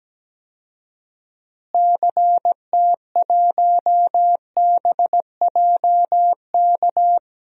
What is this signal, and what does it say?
Morse code, heard as a keyed tone, about 17 wpm, 700 Hz. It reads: CT1BJK
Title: DXCC-100 Call Signs - 6 Characters - Repeated 2x - 17wpm